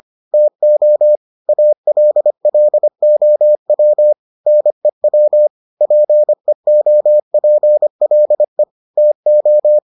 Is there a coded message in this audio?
Yes: Morse code from 0.3 to 9.8 s, heard as a keyed tone, about 25 wpm, 600 Hz. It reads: TO ALLOW NEW PEOPLE TO